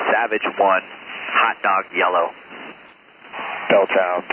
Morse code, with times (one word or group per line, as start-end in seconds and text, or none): none